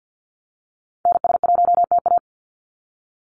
Morse code, heard as a keyed tone, about 50 wpm, 700 Hz. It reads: NS1TA
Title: NA SOTA Activator Callsigns 50wpm